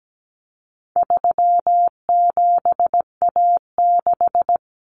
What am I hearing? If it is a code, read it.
Morse code, heard as a keyed tone, about 17 wpm, 700 Hz. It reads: 37A6
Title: Common Cut Numbers - 4-Digit 17wpm